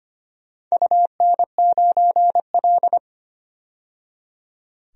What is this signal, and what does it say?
Morse code, heard as a keyed tone, about 25 wpm, 700 Hz. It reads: UN9L
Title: DXCC-100 Call Signs - 4 Characters - 25wpm